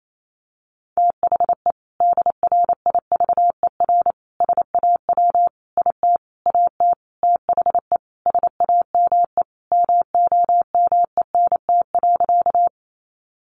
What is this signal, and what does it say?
Morse code, heard as a keyed tone, about 28 wpm, 700 Hz. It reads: THE DRIVER SAW IT AT THE SAME MOMENT.
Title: Sentences from Top 1000 Words 28wpm